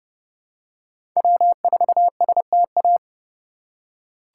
Morse code, heard as a keyed tone, about 30 wpm, 700 Hz. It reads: W4STA
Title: USA Call Signs - 5 Characters - 2x Character Spacing - 30wpm